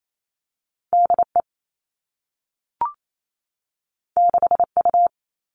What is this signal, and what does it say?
Morse code, heard as a keyed tone, about 28 wpm, 700 Hz. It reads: DE 6U
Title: All Cut Numbers - 2-Digit 28wpm